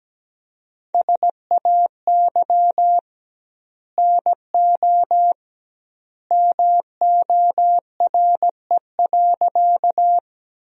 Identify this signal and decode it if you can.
Morse code, heard as a keyed tone, about 17 wpm, 700 Hz. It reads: SAY NO MORE.
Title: 2x Word-Spacing - Sentences from Top 200 Words - 17wpm